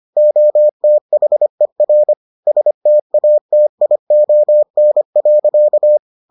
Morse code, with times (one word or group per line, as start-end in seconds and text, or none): 0.2-2.1 OTHER
2.5-6.0 STATION.